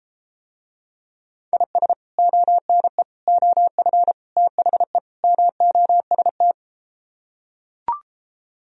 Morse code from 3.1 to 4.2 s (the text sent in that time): OF